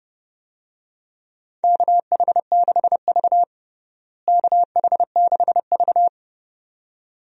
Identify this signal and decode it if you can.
Morse code, heard as a keyed tone, about 30 wpm, 700 Hz. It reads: KH6V KH6V
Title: DXCC-100 Call Signs - 4 Characters - Repeated 2x - 30wpm